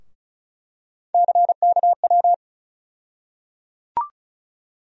Morse code, heard as a keyed tone, about 34 wpm, 700 Hz. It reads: CKW
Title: USA Call Sign Postfixes - 3 Letters 35wpm